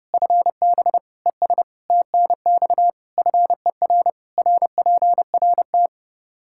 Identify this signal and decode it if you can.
Morse code, heard as a keyed tone, about 32 wpm, 700 Hz. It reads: FB ES TNX FER RPRT
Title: K1USN SST QSO Practice 30wpm